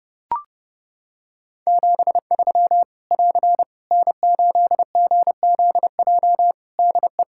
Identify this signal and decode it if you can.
Morse code, heard as a keyed tone, about 31 wpm, 700 Hz. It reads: 73 <AR> N8GZJ DE
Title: Practice QSO - Exchange 4 - Ending - 30wpm